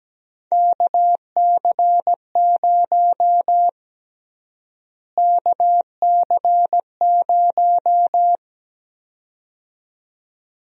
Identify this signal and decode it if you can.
Morse code, heard as a keyed tone, about 17 wpm, 700 Hz. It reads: KC0 KC0
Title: USA Call Sign Prefixes - Repeated 2x - 17wpm